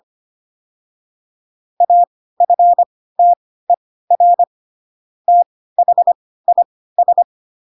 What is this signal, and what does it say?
Morse code, heard as a keyed tone, about 25 wpm, spaced as for 16 wpm, 700 Hz. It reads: AFTER THIS
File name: Pinocchio / Pinocchio ch2 10 25 ik1hge